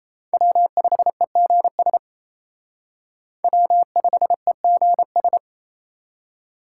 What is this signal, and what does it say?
Morse code, heard as a keyed tone, about 36 wpm, 700 Hz. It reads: W5EGS W5EGS